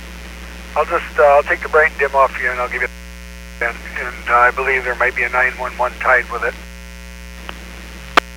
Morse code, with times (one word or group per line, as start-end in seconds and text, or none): none